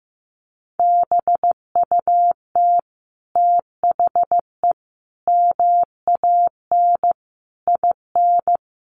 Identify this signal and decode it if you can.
Morse code, heard as a keyed tone, about 15 wpm, 700 Hz. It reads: BUT THE MAN IN
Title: Sentences from Top 600 Words 15wpm